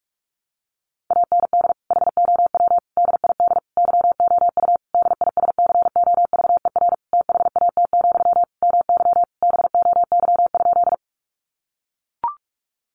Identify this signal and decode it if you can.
Morse code, heard as a keyed tone, about 44 wpm, 700 Hz. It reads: AND HOW DID YOU DISCOVER THAT, MY BOY?
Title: Sentences from Top 1100 Words 45wpm